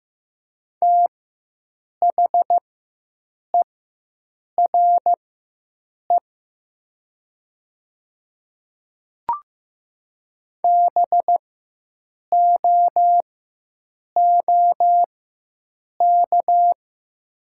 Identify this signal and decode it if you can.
Morse code, heard as a keyed tone, about 15 wpm, spaced as for 7 wpm, 700 Hz. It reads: THERE BOOK
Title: 4x Character Spacing - Top 300 Words - 15wpm